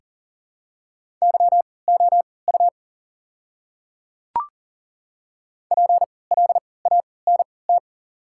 Morse code, heard as a keyed tone, about 40 wpm, spaced as for 23 wpm, 690 Hz. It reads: YOU PLANT
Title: 3x Character Spacing - Top 300 Words - 40wpm